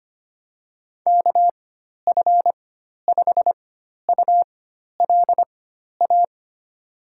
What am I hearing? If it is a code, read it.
Morse code, heard as a keyed tone, about 25 wpm, spaced as for 12 wpm, 700 Hz. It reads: KF5ULA
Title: USA Call Signs - 6 Characters - 4x Character Spacing - 25wpm